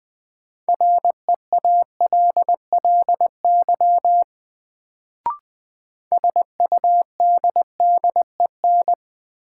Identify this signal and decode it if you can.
Morse code, heard as a keyed tone, about 20 wpm, 700 Hz. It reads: REALLY SUDDEN
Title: Sets of 4 Words - New Word Drill - 20wpm